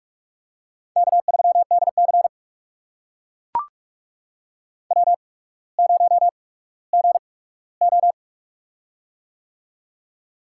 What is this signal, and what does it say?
Morse code, heard as a keyed tone, about 44 wpm, spaced as for 19 wpm, 700 Hz. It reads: K3DC W0GO
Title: USA Call Signs - 4 Characters - 8x Character Spacing - 45wpm